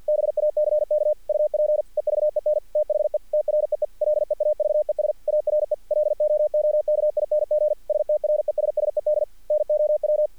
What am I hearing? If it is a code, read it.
Morse code, heard as a keyed tone, about 46 wpm, 600 Hz. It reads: BACK UP EVEN THE TREE LEAVES ARE LOOKING STRESSED NOW
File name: archived / CRSnet